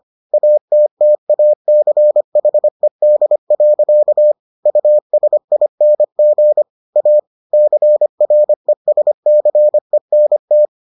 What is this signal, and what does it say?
Morse code, heard as a keyed tone, about 25 wpm, 600 Hz. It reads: ATTACHED. USING A CRESCENT